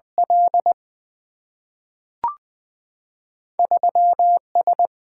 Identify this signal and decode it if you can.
Morse code, heard as a keyed tone, about 20 wpm, 700 Hz. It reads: L 3 S
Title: Single-Letter Single-Number 20wpm